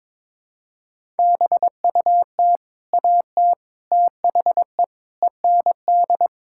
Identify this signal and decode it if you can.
Morse code, heard as a keyed tone, about 22 wpm, 700 Hz. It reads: BUT AT THE END
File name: practice / 22wpm / Sets of 8 Words - 22wpm